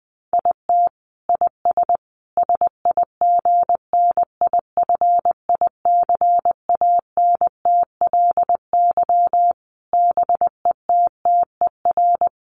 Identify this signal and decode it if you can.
Morse code, heard as a keyed tone, about 20 wpm, 700 Hz. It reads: IT IS SIGNIFICANTLY BETTER
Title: Sentences from Top 1500 Words 20wpm